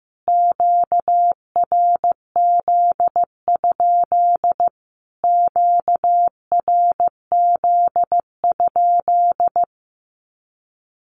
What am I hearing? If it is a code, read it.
Morse code, heard as a keyed tone, about 15 wpm, 700 Hz. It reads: QRZ? QRZ?
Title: Q-Codes Repeated 3x 15wpm